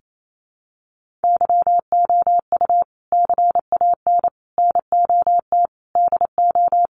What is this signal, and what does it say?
Morse code, heard as a keyed tone, about 28 wpm, 700 Hz. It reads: YOU CAN NOT DO
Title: Sets of 5 Words 28wpm